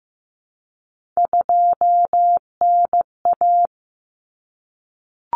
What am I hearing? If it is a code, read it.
Morse code, heard as a keyed tone, about 15 wpm, 700 Hz. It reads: 2NA
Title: Common Cut Numbers - 3-Digit 15wpm